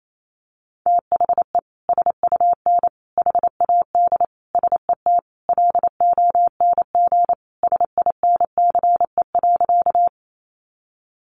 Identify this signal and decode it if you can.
Morse code, heard as a keyed tone, about 28 wpm, 700 Hz. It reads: THE SUN HAD SET LONG SINCE.
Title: Sentences from Top 300 Words 28wpm